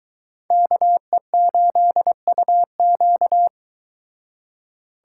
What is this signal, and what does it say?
Morse code, heard as a keyed tone, about 22 wpm, 700 Hz. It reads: KE8UQ